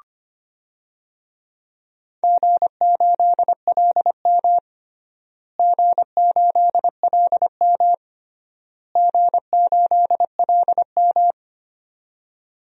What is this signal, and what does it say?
Morse code, heard as a keyed tone, about 25 wpm, 700 Hz. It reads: G8LM G8LM G8LM
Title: DXCC-100 Call Signs - 4 Characters - Repeated 3x - 25wpm